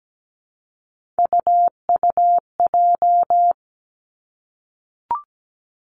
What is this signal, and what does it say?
Morse code, heard as a keyed tone, about 17 wpm, 700 Hz. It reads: UUJ
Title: Three Challenging Letter-Number Combinations 17wpm